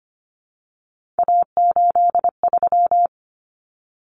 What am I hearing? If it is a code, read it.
Morse code, heard as a keyed tone, about 25 wpm, 700 Hz. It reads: A83
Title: Common Cut Numbers - 3-Digit 25wpm